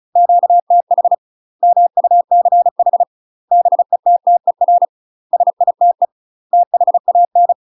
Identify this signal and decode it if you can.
Morse code, heard as a keyed tone, about 36 wpm, 700 Hz. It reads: QTH MUCH BETTER SITE THAN
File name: UK MASTER QSOs TRAINING / 35WPM / UK MASTER QSOs 0007